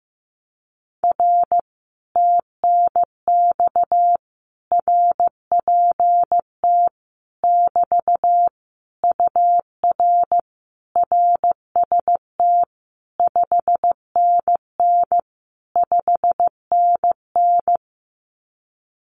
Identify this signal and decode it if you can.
Morse code, heard as a keyed tone, about 15 wpm, 700 Hz. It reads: R TNX RPT <BT> UR RST 5NN 5NN
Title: Common QSO Phrases Encore 15wpm